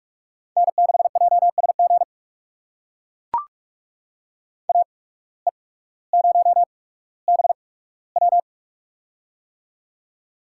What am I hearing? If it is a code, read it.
Morse code, heard as a keyed tone, about 44 wpm, spaced as for 19 wpm, 700 Hz. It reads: N6JSG AE0BW
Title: USA Call Signs - 5 Characters - 8x Character Spacing - 45wpm